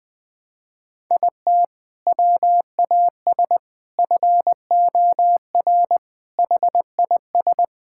Morse code, heard as a keyed tone, about 21 wpm, 700 Hz. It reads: IT WAS FOR HIS
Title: Sentences from Top 1000 Words - Review Part 6 - 20wpm